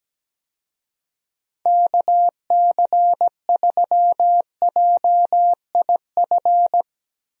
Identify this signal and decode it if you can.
Morse code, heard as a keyed tone, about 17 wpm, 700 Hz. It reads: KC3JIF
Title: USA Call Signs - 6 Characters - 17wpm